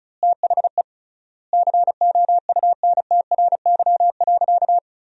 Morse code, heard as a keyed tone, about 36 wpm, 700 Hz. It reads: THE COUNTRY.
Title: 3x Word-Spacing - Sentences from Top 300 Words - 35wpm